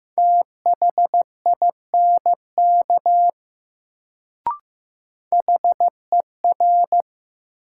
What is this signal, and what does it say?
Morse code, heard as a keyed tone, about 15 wpm, 700 Hz. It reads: THINK HER
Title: Sets of 2 Words - New Word Drill - 15wpm